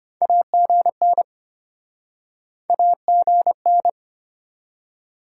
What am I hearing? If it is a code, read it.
Morse code, heard as a keyed tone, about 37 wpm, spaced as for 32 wpm, 700 Hz. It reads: AGN AGN